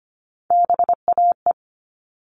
Morse code, heard as a keyed tone, about 25 wpm, 700 Hz. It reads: BAE
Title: All Cut Numbers - 3-Digit 25wpm